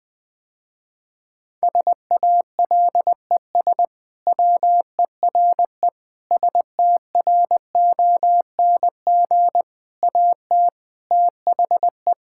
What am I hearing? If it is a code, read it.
Morse code, heard as a keyed tone, about 20 wpm, 700 Hz. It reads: SALES WERE STRONG AT THE
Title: Sentences with 6-Word Phrases - Spoken Beginning - 20wpm